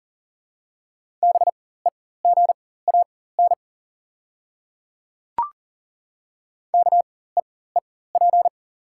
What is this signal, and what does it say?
Morse code, heard as a keyed tone, about 37 wpm, spaced as for 18 wpm, 700 Hz. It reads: BEGAN KEEP
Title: 4x Character Spacing - Top 300 Words - 40wpm